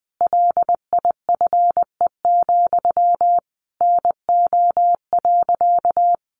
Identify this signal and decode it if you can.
Morse code, heard as a keyed tone, about 20 wpm, 700 Hz. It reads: LIFE, NO.
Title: Sentences from Top 300 Words 20wpm